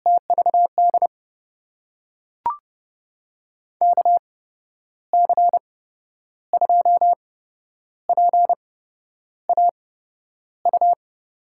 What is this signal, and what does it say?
Morse code, heard as a keyed tone, about 30 wpm, spaced as for 10 wpm, 700 Hz. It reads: TVD KC2PAU
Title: USA Call Signs - 6 Characters - 8x Character Spacing - 30wpm